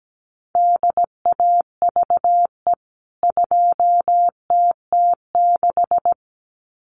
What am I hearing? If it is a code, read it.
Morse code, heard as a keyed tone, about 17 wpm, 700 Hz. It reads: DAVE 2TT6
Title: CWT Exchange 17wpm